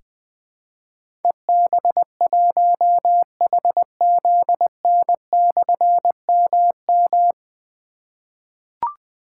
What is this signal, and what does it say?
Morse code, heard as a keyed tone, about 20 wpm, 700 Hz. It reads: EB1HZN/MM
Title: Difficult Call Signs 20wpm